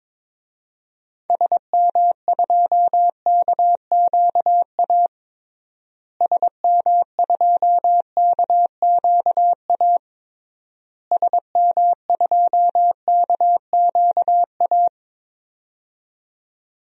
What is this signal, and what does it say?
Morse code, heard as a keyed tone, about 23 wpm, 700 Hz. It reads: SM2KQA SM2KQA SM2KQA
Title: DXCC-100 Call Signs - 6 Characters - Repeated 3x - 22wpm